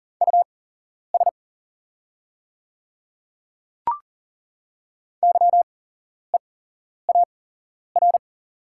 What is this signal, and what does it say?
Morse code, heard as a keyed tone, about 40 wpm, spaced as for 11 wpm, 700 Hz. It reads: US YEAR
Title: 8x Character Spacing - Top 200 Words - 40wpm